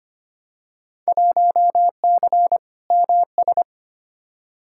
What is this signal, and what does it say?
Morse code, heard as a keyed tone, about 25 wpm, 700 Hz. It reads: 1C MS
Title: Field Day Exchanges Encore - 25wpm